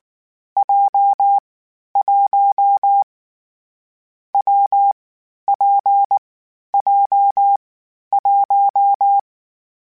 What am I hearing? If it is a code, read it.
Morse code, heard as a keyed tone, about 19 wpm, 800 Hz. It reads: J 1 W P J 1
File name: Difficult Combos / Difficult Combos-mp3s / WPJ WPJ1 @18